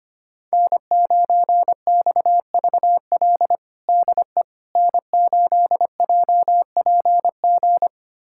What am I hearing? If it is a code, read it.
Morse code, heard as a keyed tone, about 25 wpm, 700 Hz. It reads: N9XVL DE N8JPG
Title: USA Callsign DE Callsign - 5 Characters - 25wpm